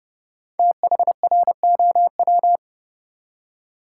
Morse code, heard as a keyed tone, about 30 wpm, 700 Hz. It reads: THROW
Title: Five Letter Words - 30wpm